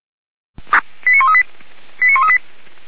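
Five-tone selective call (ZVEI), heard as quick runs of five tones, tones of about 80 ms. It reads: 89138
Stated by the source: ZVEI five-tone call 89132 received by SDR, Feuerwehr Orlamünde.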